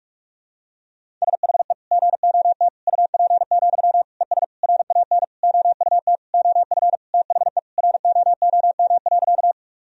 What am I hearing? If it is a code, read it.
Morse code, heard as a keyed tone, about 42 wpm, 700 Hz. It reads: SHE GOT UP, ES RAN OUT OF THE ROOM.
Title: Sentences from Top 500 Words and Abbreviations 45wpm